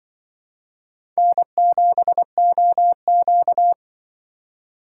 Morse code, heard as a keyed tone, about 26 wpm, 700 Hz. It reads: N7OQ